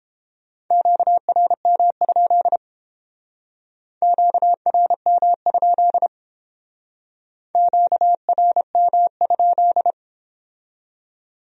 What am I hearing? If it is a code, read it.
Morse code, heard as a keyed tone, about 30 wpm, 700 Hz. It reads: QRM? QRM? QRM?